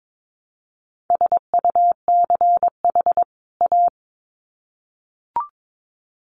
Morse code, heard as a keyed tone, about 22 wpm, 700 Hz. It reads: SUCH A
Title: Sets of 2 Words 22wpm